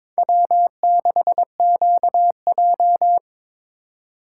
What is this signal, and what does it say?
Morse code, heard as a keyed tone, about 22 wpm, 700 Hz. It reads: W6QJ
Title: USA Call Signs - 4 Characters - 3x Character Spacing - 22wpm